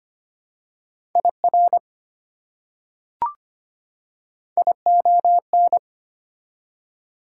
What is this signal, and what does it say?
Morse code, heard as a keyed tone, about 25 wpm, 700 Hz. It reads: IR ION
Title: Prefixes and Suffixes 25wpm